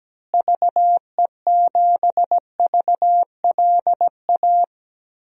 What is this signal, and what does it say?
Morse code, heard as a keyed tone, about 17 wpm, 700 Hz. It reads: VE7VLA
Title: DXCC-100 Call Signs - 6 Characters - Repeated 2x - 17wpm